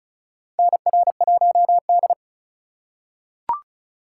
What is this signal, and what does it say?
Morse code, heard as a keyed tone, about 36 wpm, 700 Hz. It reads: NR1D
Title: USA Call Signs - 4 Characters - 4x Character Spacing - 35wpm